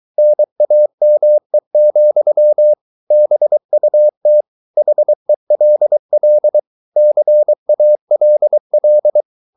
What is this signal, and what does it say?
Morse code, heard as a keyed tone, about 23 wpm, 600 Hz. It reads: NAME, BUT HELL CALL